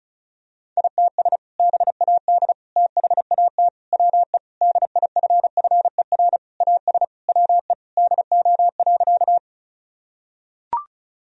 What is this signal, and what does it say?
Morse code, heard as a keyed tone, about 34 wpm, 690 Hz. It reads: ITS BAD THAT WE DIFFER AS WE DO.
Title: Sentences from Top 1000 Words - Review Part 1 - 35wpm